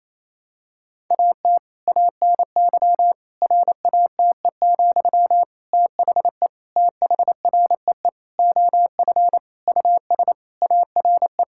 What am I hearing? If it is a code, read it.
Morse code, heard as a keyed tone, about 28 wpm, 700 Hz. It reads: AT ANY RATE, THE THREE OF US ARE
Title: Sentences from Top 1000 Words - Review Part 6 - 28wpm